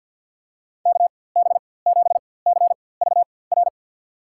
Morse code, heard as a keyed tone, about 51 wpm, spaced as for 23 wpm, 700 Hz. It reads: KB7CVR
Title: USA Call Signs - 6 Characters - 4x Character Spacing - 50wpm